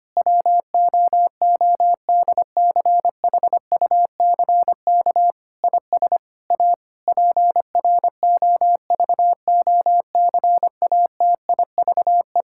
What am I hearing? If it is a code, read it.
Morse code, heard as a keyed tone, about 25 wpm, 700 Hz. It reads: WOODCHUCK IS A PROVOCATIVE
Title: Pangram Challenge v2 - 25wpm